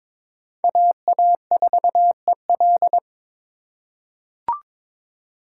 Morse code, heard as a keyed tone, about 22 wpm, 700 Hz. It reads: AA4EL
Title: USA Call Signs - 3x Character Spacing - 22wpm